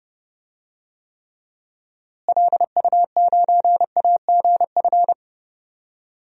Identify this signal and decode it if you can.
Morse code, heard as a keyed tone, about 30 wpm, 700 Hz. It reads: LU9AGF
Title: DXCC-100 Call Signs - 6 Characters - Repeated 2x - 30wpm